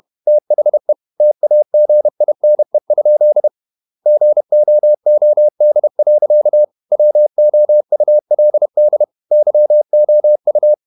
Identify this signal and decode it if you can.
Morse code, heard as a keyed tone, about 31 wpm, 600 Hz. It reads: THE TAGINE? GOOD. WOULD YOU
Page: Never 31wpm | CW med Gnister